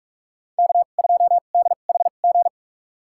Morse code, heard as a keyed tone, about 46 wpm, spaced as for 32 wpm, 690 Hz. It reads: K2DHG
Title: USA Call Signs - 5 Characters - 2x Character Spacing - 45wpm